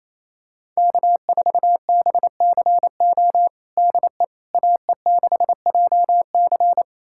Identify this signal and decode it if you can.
Morse code, heard as a keyed tone, about 27 wpm, 700 Hz. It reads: K4BCO DE AE6JC
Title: USA Callsign DE Callsign - 5 Characters - 28wpm